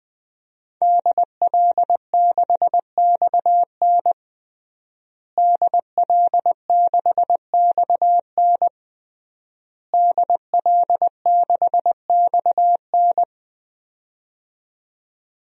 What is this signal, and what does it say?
Morse code, heard as a keyed tone, about 20 wpm, 700 Hz. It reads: DL6XN DL6XN DL6XN
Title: DXCC-100 Call Signs - 5 Characters - Repeated 3x - 20wpm